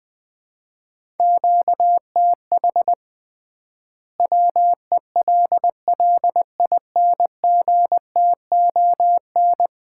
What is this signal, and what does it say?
Morse code, heard as a keyed tone, about 21 wpm, 700 Hz. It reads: QTH WELLINGTON
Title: Common QSO Phrases - 3x Word-Spacing - 20wpm